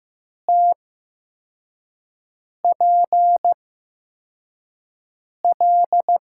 Morse code, heard as a keyed tone, about 15 wpm, spaced as for 4 wpm, 700 Hz. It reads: TPL
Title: USA Call Signs - 8x Character Spacing - 15wpm